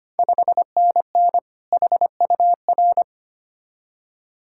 Morse code, heard as a keyed tone, about 25 wpm, 700 Hz. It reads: 5NN HUR
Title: Ontario QSO Party Exchange 25wpm